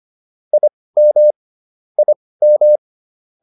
Morse code, heard as a keyed tone, about 25 wpm, spaced as for 18 wpm, 600 Hz. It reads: IM IM